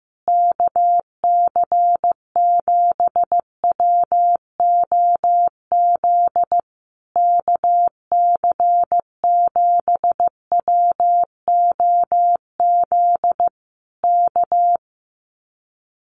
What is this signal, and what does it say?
Morse code, heard as a keyed tone, about 15 wpm, 700 Hz. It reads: KC7WOZ KC7WOZ K
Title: Common QSO Phrases Encore 15wpm